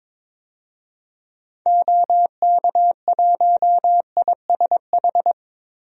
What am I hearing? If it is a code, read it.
Morse code, heard as a keyed tone, about 22 wpm, 700 Hz. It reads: OK1ISH
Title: DXCC-100 Call Signs - 6 Characters - 22wpm